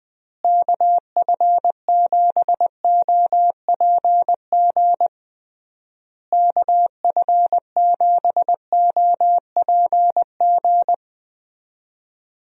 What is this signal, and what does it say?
Morse code, heard as a keyed tone, about 20 wpm, 700 Hz. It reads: KF7OPG KF7OPG
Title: USA Call Signs - 6 Characters - Repeated 3x - 20wpm